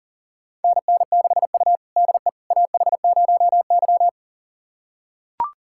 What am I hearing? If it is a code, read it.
Morse code, heard as a keyed tone, about 38 wpm, 700 Hz. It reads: NN6U DE AH0Y
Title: USA Callsign DE Callsign - 4 Characters - 40wpm